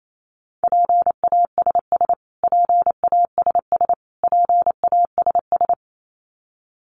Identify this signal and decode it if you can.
Morse code, heard as a keyed tone, about 28 wpm, 700 Hz. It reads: PASS PASS PASS
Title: Top 500 Words Repeated 28wpm